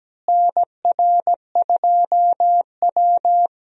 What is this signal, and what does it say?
Morse code, heard as a keyed tone, about 17 wpm, 700 Hz. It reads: NR2W
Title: USA Call Signs - 4 Characters - Repeated 3x - 17wpm